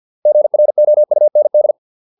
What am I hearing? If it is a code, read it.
Morse code, heard as a keyed tone, about 50 wpm, 600 Hz. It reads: GROUND